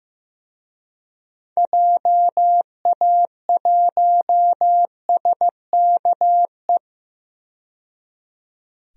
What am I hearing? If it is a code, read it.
Morse code, heard as a keyed tone, about 15 wpm, 700 Hz. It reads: JA1SKE
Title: DXCC-100 Call Signs - 6 Characters - 15wpm